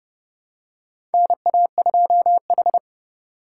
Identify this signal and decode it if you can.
Morse code, heard as a keyed tone, about 32 wpm, 700 Hz. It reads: NA2H
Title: USA Call Signs - 3x Character Spacing - 30wpm